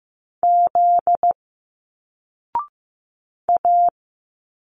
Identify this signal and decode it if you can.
Morse code, heard as a keyed tone, about 15 wpm, 700 Hz. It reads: Z A
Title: Single Letters 15wpm